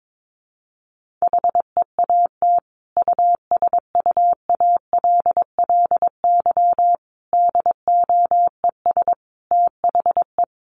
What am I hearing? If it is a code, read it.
Morse code, heard as a keyed tone, about 22 wpm, 700 Hz. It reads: HEAT USUALLY DOES THE
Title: Sentences from Top 1500 Words 22wpm